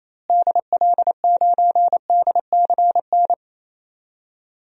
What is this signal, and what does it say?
Morse code, heard as a keyed tone, about 27 wpm, 700 Hz. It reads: DL9DCN
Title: DXCC-100 Call Signs - 6 Characters - Repeated 3x - 28wpm